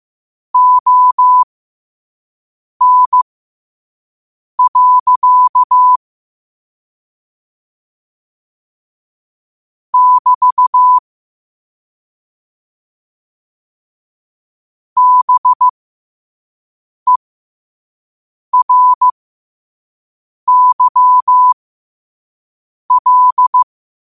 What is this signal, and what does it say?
Morse code, heard as a keyed tone, about 15 wpm, spaced as for 5 wpm, 1000 Hz. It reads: ON. <BT> BERYL
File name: qotdfiles / 5wpm